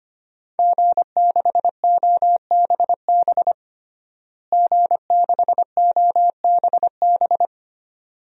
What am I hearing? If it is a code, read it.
Morse code, heard as a keyed tone, about 25 wpm, 700 Hz. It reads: G6OBB G6OBB
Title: DXCC-100 Call Signs - 5 Characters - Repeated 2x - 25wpm